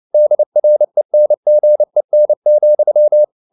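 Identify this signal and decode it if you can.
Morse code, heard as a keyed tone, about 29 wpm, 600 Hz. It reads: DRENGEN,
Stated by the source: Fast (QRQ) Morse at 29 wpm, from CW med Gnister.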